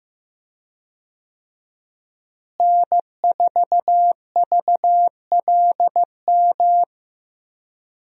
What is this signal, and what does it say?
Morse code, heard as a keyed tone, about 15 wpm, 700 Hz. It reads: N4VLM
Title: USA Call Signs - 5 Characters - Repeated 2x - 15wpm